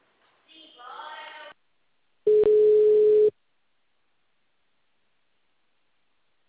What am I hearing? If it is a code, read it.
Ringback tone (Europe, 425 Hz).